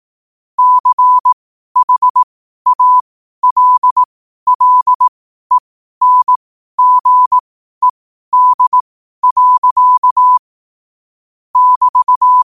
Morse code, heard as a keyed tone, about 18 wpm, spaced as for 12 wpm, 1000 Hz. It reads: CHALLENGED. <BT>
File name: qotdfiles / 12wpm